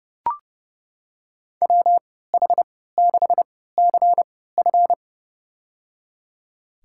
Morse code, heard as a keyed tone, about 30 wpm, spaced as for 17 wpm, 700 Hz. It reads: WH6CF
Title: USA Call Signs - 3x Character Spacing - 30wpm